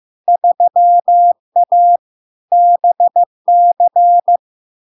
Morse code, heard as a keyed tone, about 15 wpm, 700 Hz. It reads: 3A BC
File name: LARK-FD-2022-Log-in-CW-15wpm-700-Hz